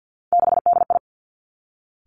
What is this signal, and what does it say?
Morse code, heard as a keyed tone, about 50 wpm, 700 Hz. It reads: 6DI
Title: Three Challenging Letter-Number Combinations 50wpm